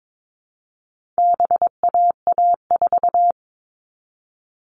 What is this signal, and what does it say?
Morse code, heard as a keyed tone, about 22 wpm, 700 Hz. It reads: BAA4
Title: All Cut Numbers - 4-Digit 22wpm